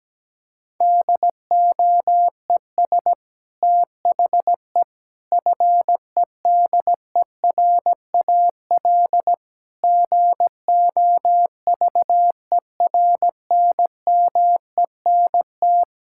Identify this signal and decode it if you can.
Morse code, heard as a keyed tone, about 17 wpm, 700 Hz. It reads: DOES THE FEDERAL GOVERNMENT
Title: Sentences from Top 1000 Words - Review Part 5 - 17wpm